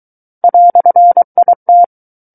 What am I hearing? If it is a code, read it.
Morse code, heard as a keyed tone, about 23 wpm, 700 Hz. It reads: "IT
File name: Book One / 14-Chapter 14 - 23 WPM